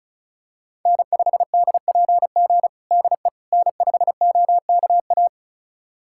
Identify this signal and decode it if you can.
Morse code, heard as a keyed tone, about 34 wpm, 700 Hz. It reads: N5DPG DE N5OKA
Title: USA Callsign DE Callsign - 5 Characters - 35wpm